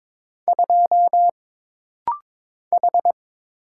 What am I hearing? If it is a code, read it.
Morse code, heard as a keyed tone, about 22 wpm, 700 Hz. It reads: 2 H
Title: Single Letter-Number - Rapid-Fire 22wpm